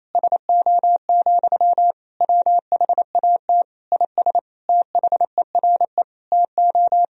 Morse code, heard as a keyed tone, about 28 wpm, 700 Hz. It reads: SO, WHAT IS THERE TO
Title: Sentences from Top 200 Words and Abbreviations 28wpm